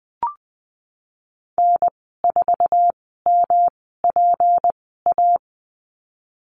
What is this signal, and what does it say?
Morse code, heard as a keyed tone, about 20 wpm, spaced as for 15 wpm, 700 Hz. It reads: N4MPA
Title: 179 Lesson 29g - Slash - Call Signs 20wpm